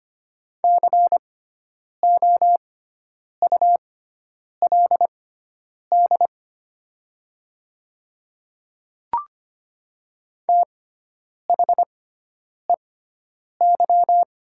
Morse code, heard as a keyed tone, about 25 wpm, spaced as for 9 wpm, 700 Hz. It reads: COULD THEY
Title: 6x Character Spacing - Top 100 Words - 25wpm